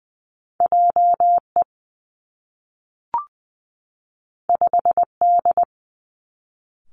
Morse code, heard as a keyed tone, about 20 wpm, 700 Hz. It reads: J E 5 D
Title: Two Challenging Letter-Number Combinations 20wpm